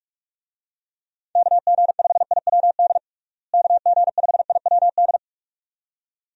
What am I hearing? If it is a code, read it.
Morse code, heard as a keyed tone, about 42 wpm, 700 Hz. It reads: KG5IWD KG5IWD
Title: USA Call Signs - 6 Characters - Repeated 2x - 45wpm